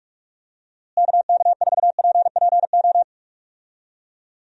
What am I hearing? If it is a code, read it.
Morse code, heard as a keyed tone, about 48 wpm, 700 Hz. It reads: KK4PPO
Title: USA Call Signs - 6 Characters - Repeated 2x - 45wpm